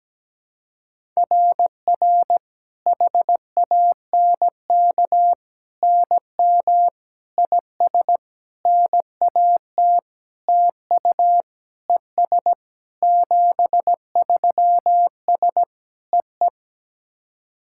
Morse code, heard as a keyed tone, about 17 wpm, 700 Hz. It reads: RR HANK NM IS NAT TU ES 73S EE
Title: SOTA QSO Practice 17wpm